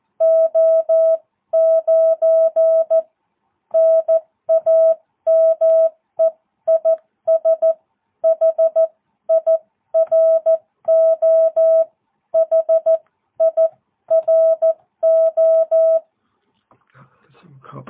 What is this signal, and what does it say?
Morse code, heard as a keyed tone, about 14 wpm, spaced as for 12 wpm, 650 Hz. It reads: O9 NAME IS HIRO HIRO
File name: jh8cbh / 2022cw